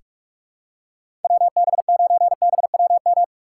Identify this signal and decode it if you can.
Morse code, heard as a keyed tone, about 46 wpm, 700 Hz. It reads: WB9BWM
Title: USA Call Signs - 6 Characters - 2x Character Spacing - 45wpm